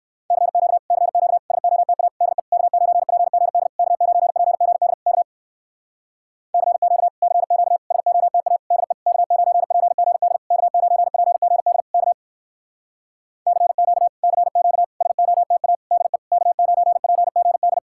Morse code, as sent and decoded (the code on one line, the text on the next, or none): -.-. --.- / -.-. --.- / ... --- - .- / -.. / -.- ----. .--. --. -.. / -.- ----. .--. --. -.. / -.- / -.-. --.- / -.-. --.- / ... --- - .- / -.. / -.- ----. .--. --. -.. / -.- ----. .--. --. -.. / -.- / -.-. --.- / -.-. --.- / ... --- - .- / -.. / -.- ----. .--. --. -..
CQ CQ SOTA DE K9PGD K9PGD K CQ CQ SOTA DE K9PGD K9PGD K CQ CQ SOTA DE K9PGD